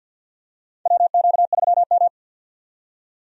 Morse code, heard as a keyed tone, about 46 wpm, 700 Hz. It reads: WQ3M
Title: USA Call Signs - 4 Characters - 50wpm